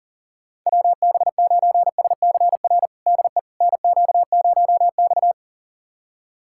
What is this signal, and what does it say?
Morse code, heard as a keyed tone, about 40 wpm, 700 Hz. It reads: WB9SCR DE NQ0X
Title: USA Callsign DE Callsign 40wpm